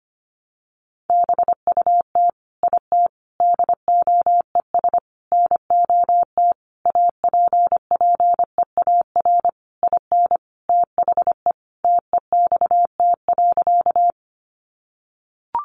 Morse code, heard as a keyed tone, about 25 wpm, 700 Hz. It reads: BUT IT DOES NOT APPEAR IN THE TEXT.
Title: Sentences from Top 1200 Words 25wpm